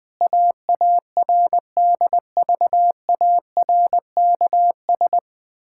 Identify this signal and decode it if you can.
Morse code, heard as a keyed tone, about 20 wpm, 700 Hz. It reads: AARDVARKS
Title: Pangram Challenge v2 - 20wpm